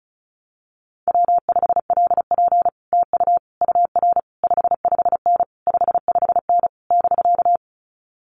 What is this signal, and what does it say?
Morse code, heard as a keyed tone, about 36 wpm, 700 Hz. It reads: W5LP TU UR 55N 55N <BK>
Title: Common QSO Phrases Encore 35wpm